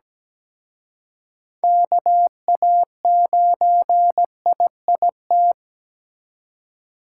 Morse code, heard as a keyed tone, about 20 wpm, 700 Hz. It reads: KA9IIT
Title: Speed Racing - USA 2x3 Call Signs 17wpm